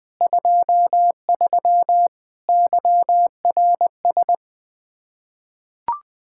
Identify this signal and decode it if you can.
Morse code, heard as a keyed tone, about 20 wpm, 700 Hz. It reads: 23 YRS
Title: Practice QSO - Exchange 3 - More Info - 20wpm